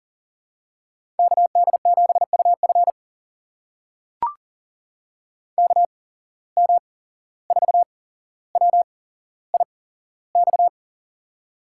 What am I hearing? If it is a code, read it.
Morse code, heard as a keyed tone, about 40 wpm, spaced as for 16 wpm, 700 Hz. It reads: KD7UF KM4WIX
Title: USA Call Signs - 8x Character Spacing - 40wpm